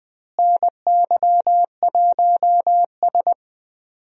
Morse code, heard as a keyed tone, about 21 wpm, 700 Hz. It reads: NY1S
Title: USA Call Signs - 4 Characters - Repeated 2x - 20wpm